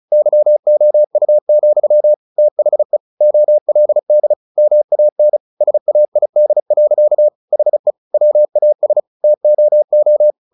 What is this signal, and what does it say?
Morse code, heard as a keyed tone, about 34 wpm, 600 Hz. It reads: YOU, THE OLD MAN SAID. HE WAS TOO
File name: the old man and the sea / the old man and the sea 1 35wpm